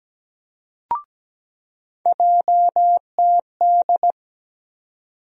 Morse code, heard as a keyed tone, about 17 wpm, 700 Hz. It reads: JTD
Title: USA Call Sign Postfixes - 3 Letters 17wpm